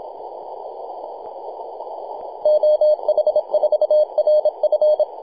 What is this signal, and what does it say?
Morse code, heard as a keyed tone, about 27 wpm, 610 Hz. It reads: OH4RF
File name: t88zo-28-008mhzcw